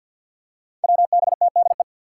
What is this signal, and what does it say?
Morse code, heard as a keyed tone, about 48 wpm, 700 Hz. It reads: WBTDE
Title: Five Challenging Letter-Number Combinations 50wpm